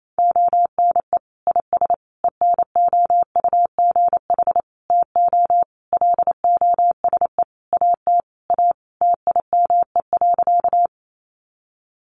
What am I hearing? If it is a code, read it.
Morse code, heard as a keyed tone, about 28 wpm, 700 Hz. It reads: ONE IS ENOUGH TO LOSE AT A TIME.
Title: Sentences from Top 1100 Words 28wpm